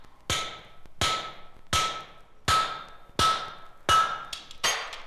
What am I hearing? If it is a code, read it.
There is no code.